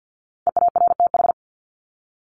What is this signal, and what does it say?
Morse code, heard as a keyed tone, about 50 wpm, 700 Hz. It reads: EARTH
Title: Top 300 Words Repeated 50wpm